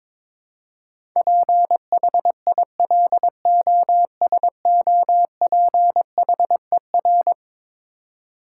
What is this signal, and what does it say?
Morse code, heard as a keyed tone, about 22 wpm, 700 Hz. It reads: PHILOSOPHER
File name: practice / 22wpm / Eleven Letter Words - 22wpm